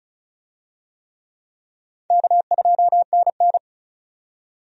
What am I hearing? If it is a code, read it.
Morse code, heard as a keyed tone, about 35 wpm, 700 Hz. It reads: K2NN
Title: USA Call Signs - 4 Characters - 35wpm